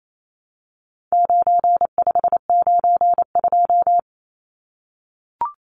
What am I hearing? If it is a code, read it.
Morse code, heard as a keyed tone, about 28 wpm, 700 Hz. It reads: 9592
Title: Numbers - 4-Digit 28wpm